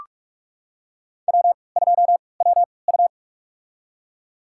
Morse code, heard as a keyed tone, about 46 wpm, spaced as for 25 wpm, 700 Hz. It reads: W2WU
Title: USA Call Signs - 3x Character Spacing - 45wpm